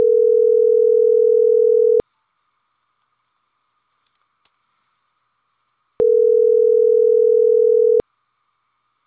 Ringback tone (North America), 440 + 480 Hz.